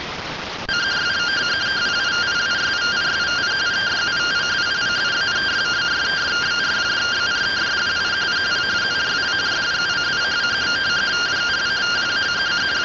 Two-tone teleprinter signal, about 170 Hz shift.